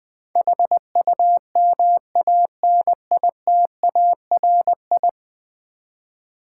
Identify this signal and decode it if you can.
Morse code, heard as a keyed tone, about 20 wpm, 700 Hz. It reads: HUMANITARI
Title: Build A Word - 5 Letters At A Time - 12-Letter Words - 20wpm